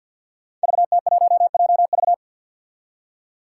Morse code, heard as a keyed tone, about 48 wpm, 700 Hz. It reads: VT1JV
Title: Five Challenging Letter-Number Combinations 50wpm